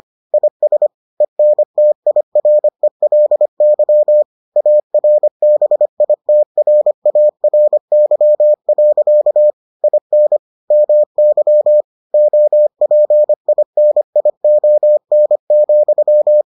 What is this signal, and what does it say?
Morse code, heard as a keyed tone, about 25 wpm, 600 Hz. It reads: IS ENTIRELY ARBITRARY. IN MY OPINION,